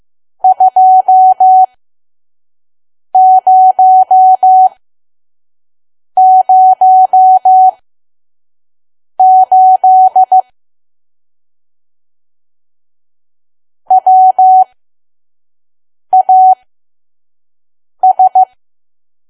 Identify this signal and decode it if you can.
Morse code, heard as a keyed tone, about 15 wpm, spaced as for 5 wpm, 750 Hz. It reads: 2008 WAS
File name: Archive / 5 WPM files / PM